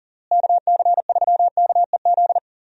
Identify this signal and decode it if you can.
Morse code, heard as a keyed tone, about 40 wpm, 700 Hz. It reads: KC3KEZ
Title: USA Call Signs - 6 Characters - Repeated 3x - 40wpm